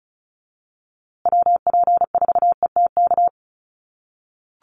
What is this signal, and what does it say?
Morse code, heard as a keyed tone, about 34 wpm, 700 Hz. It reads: WP4ETK
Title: USA Call Signs Repeated 2x 35wpm